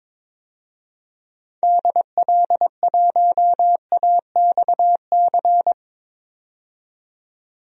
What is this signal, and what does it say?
Morse code, heard as a keyed tone, about 22 wpm, 700 Hz. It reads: DL1AXC
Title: DXCC-100 Call Signs - 6 Characters - 22wpm